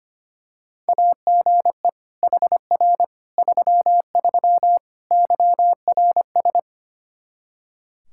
Morse code, heard as a keyed tone, about 25 wpm, 700 Hz. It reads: AGE HR 33 YRS
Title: K1USN SST QSO Practice 25wpm